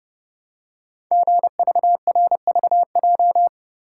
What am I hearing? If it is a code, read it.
Morse code, heard as a keyed tone, about 30 wpm, 700 Hz. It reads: GVRVJ
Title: Code Groups - 5 Letter:Numbers - 30wpm